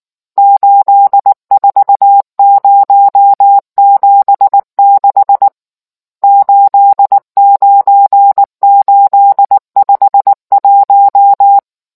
Morse code, heard as a keyed tone, about 19 wpm, 800 Hz. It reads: 84076 89851